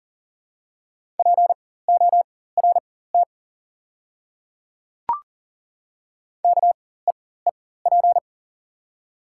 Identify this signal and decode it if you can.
Morse code, heard as a keyed tone, about 40 wpm, spaced as for 19 wpm, 700 Hz. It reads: PORT KEEP
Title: 4x Character Spacing - Top 300 Words - 40wpm